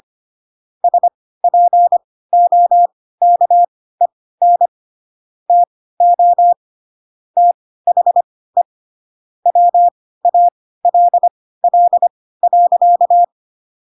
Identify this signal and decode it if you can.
Morse code, heard as a keyed tone, about 25 wpm, spaced as for 16 wpm, 700 Hz. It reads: SPOKEN TO THE WALL.
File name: Pinocchio / Pinocchio ch3 10 25 ik1hge